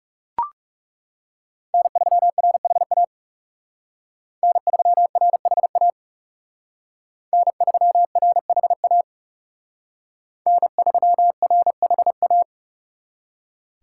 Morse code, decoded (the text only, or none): N3RHA N3RHA N3RHA N3RHA